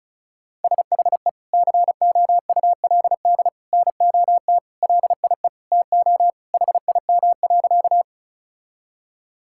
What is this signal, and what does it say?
Morse code, heard as a keyed tone, about 36 wpm, 700 Hz. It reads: SHE COULD NOT LIE TO HIM.
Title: Sentences from Top 1600 Words 35wpm